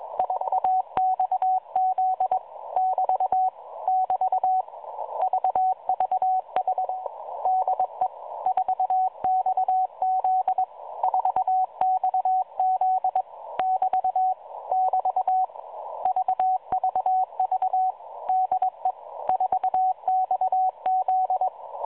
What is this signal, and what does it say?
Morse code, heard as a keyed tone, about 22 wpm, 750 Hz, with radio noise. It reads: IAXZ <BT> <BT> VVH DE 4XZ 4XZ <BT> <BT> VVV DE 4XZ
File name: 4XZ Shazepe